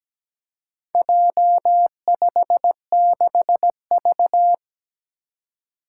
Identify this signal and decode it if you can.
Morse code, heard as a keyed tone, about 17 wpm, 700 Hz. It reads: J56V
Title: Four Challenging Letter-Number Combinations 17wpm